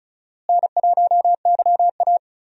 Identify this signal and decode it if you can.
Morse code, heard as a keyed tone, about 34 wpm, 700 Hz. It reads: N1YA
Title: USA Call Signs - 4 Characters - Repeated 3x - 35wpm